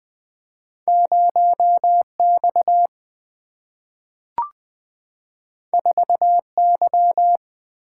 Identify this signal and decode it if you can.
Morse code, heard as a keyed tone, about 20 wpm, 700 Hz. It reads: 0 X 4 Y
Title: Single-Letter Single-Number 20wpm